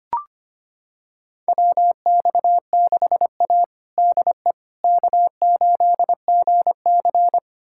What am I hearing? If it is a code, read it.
Morse code, heard as a keyed tone, about 25 wpm, 700 Hz. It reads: WX6A DE K8GC
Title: USA Callsign DE Callsign - 4 Characters - 25wpm